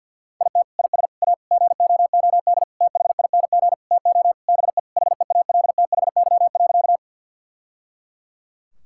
Morse code, heard as a keyed tone, about 49 wpm, 700 Hz. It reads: IT IS A GOOD THING TO BE HEALTHY.
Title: Sentences from Top 1000 Words - Review Part 5 - 50wpm